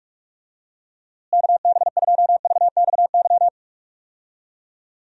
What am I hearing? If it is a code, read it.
Morse code, heard as a keyed tone, about 41 wpm, 700 Hz. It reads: KB2VXY
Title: USA Call Signs - 6 Characters - Repeated 3x - 45wpm